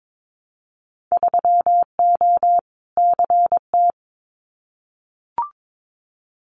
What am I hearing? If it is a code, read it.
Morse code, heard as a keyed tone, about 22 wpm, 700 Hz. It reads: 3O CT
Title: Winter Field Day Exchanges Encore 22wpm